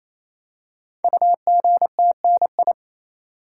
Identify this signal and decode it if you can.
Morse code, heard as a keyed tone, about 27 wpm, 700 Hz. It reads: UGTNI
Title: Code Groups - 5 Letters - 28wpm